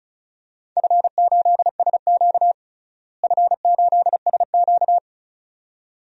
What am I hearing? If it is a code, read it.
Morse code, heard as a keyed tone, about 34 wpm, 700 Hz. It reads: F8SQ F8SQ
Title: DXCC-100 Call Signs - 4 Characters - Repeated 2x - 35wpm